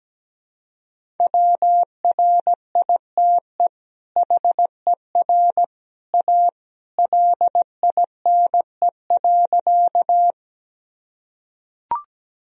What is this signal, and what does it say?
Morse code, heard as a keyed tone, about 17 wpm, 700 Hz. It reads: WRITE HER A LINE.
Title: Sentences from Top 1000 Words - Review Part 1 - 17wpm